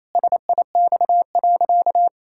Morse code, heard as a keyed tone, about 28 wpm, 700 Hz. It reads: SIX.
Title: Pangram Challenge v2 - 28wpm